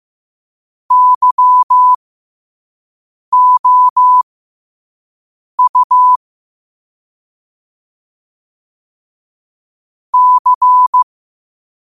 Morse code, heard as a keyed tone, about 15 wpm, spaced as for 5 wpm, 1000 Hz. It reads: YOU C
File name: qotdfiles / 5wpm